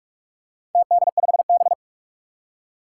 Morse code, heard as a keyed tone, about 46 wpm, 700 Hz. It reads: TD5B
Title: Four Challenging Letter-Number Combinations 45wpm